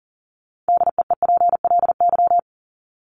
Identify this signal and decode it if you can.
Morse code, heard as a keyed tone, about 40 wpm, 700 Hz. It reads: DEEPLY
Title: LY Suffix - Top 100 Words with Base Words 40wpm